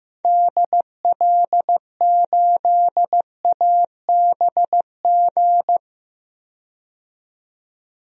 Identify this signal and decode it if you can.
Morse code, heard as a keyed tone, about 15 wpm, 700 Hz. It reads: DL8ABG
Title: DXCC-100 Call Signs - 6 Characters - 15wpm